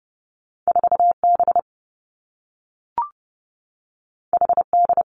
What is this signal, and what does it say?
Morse code, heard as a keyed tone, about 30 wpm, 700 Hz. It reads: 4 B H D
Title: Two Challenging Letter-Number Combinations 30wpm